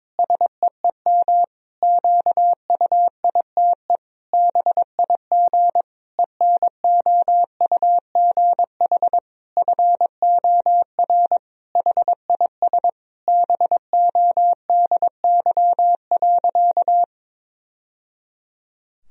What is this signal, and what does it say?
Morse code, heard as a keyed tone, about 22 wpm, 700 Hz. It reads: SEEM QUITE BIG ENOUGH FOR HIS BODY.
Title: Sentences from Top 1000 Words - Review Part 2 - 22wpm